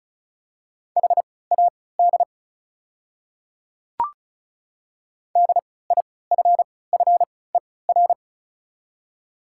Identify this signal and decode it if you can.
Morse code, heard as a keyed tone, about 34 wpm, spaced as for 20 wpm, 690 Hz. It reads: HAD DIFFER
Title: 3x Character Spacing - Top 500 Words - 35wpm